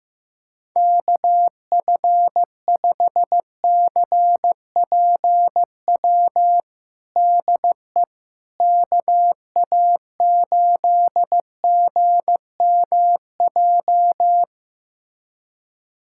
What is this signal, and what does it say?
Morse code, heard as a keyed tone, about 15 wpm, 700 Hz. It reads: KF5CPW DE KA8GMJ
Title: USA Callsign DE Callsign - 6 Characters - 15wpm